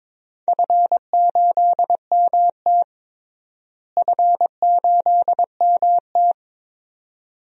Morse code, heard as a keyed tone, about 22 wpm, 700 Hz. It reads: F8MT F8MT
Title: DXCC-100 Call Signs - 4 Characters - Repeated 2x - 22wpm